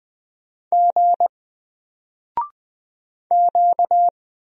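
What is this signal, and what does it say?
Morse code, heard as a keyed tone, about 20 wpm, 700 Hz. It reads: G Q
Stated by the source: Single letter and number Morse drill at 20 wpm, from N4LCD.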